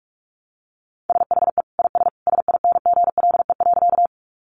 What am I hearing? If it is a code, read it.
Morse code, heard as a keyed tone, about 44 wpm, 700 Hz. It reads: SHE IS SINGLE.
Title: Sentences from Top 700 Words 45wpm